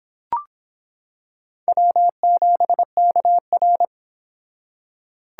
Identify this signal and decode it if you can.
Morse code, heard as a keyed tone, about 26 wpm, 700 Hz. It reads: W7KR